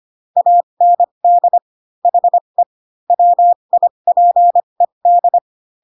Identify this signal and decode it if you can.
Morse code, heard as a keyed tone, about 25 wpm, spaced as for 22 wpm, 700 Hz. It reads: AND HE WIPED
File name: Pinocchio / Pinocchio ch3 18 25 ik1hge